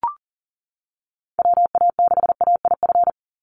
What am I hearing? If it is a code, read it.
Morse code, heard as a keyed tone, about 40 wpm, 690 Hz. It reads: WA6AIF
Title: USA Call Signs 40wpm